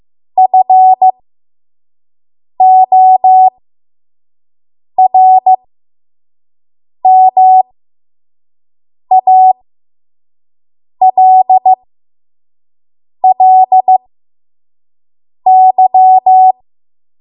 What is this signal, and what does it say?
Morse code, heard as a keyed tone, about 15 wpm, spaced as for 5 wpm, 750 Hz. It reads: FORMALLY
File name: Morse / 05WPM